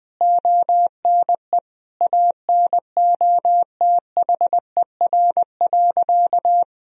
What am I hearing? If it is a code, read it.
Morse code, heard as a keyed tone, about 20 wpm, 700 Hz. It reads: ONE ANOTHER.
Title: Sentences from Top 1000 Words - Review Part 1 - 20wpm